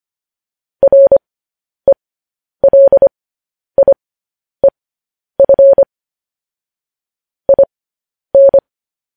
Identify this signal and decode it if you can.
Morse code, heard as a keyed tone, about 25 wpm, spaced as for 10 wpm, 550 Hz. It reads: RELIEF IN